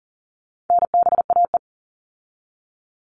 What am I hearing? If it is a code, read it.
Morse code, heard as a keyed tone, about 40 wpm, 700 Hz. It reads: NBAE
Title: All Cut Numbers - 4-Digit 40wpm